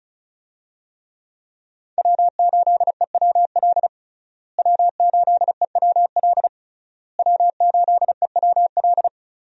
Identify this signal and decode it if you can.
Morse code, heard as a keyed tone, about 36 wpm, 700 Hz. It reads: W8EWL W8EWL W8EWL
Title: USA Call Signs - 5 Characters - Repeated 3x - 35wpm